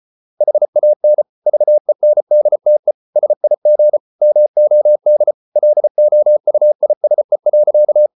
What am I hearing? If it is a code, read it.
Morse code, heard as a keyed tone, about 34 wpm, 600 Hz. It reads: HAN VENDTE SIG MOD LOUISE.
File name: Ferie QRQ 34wpm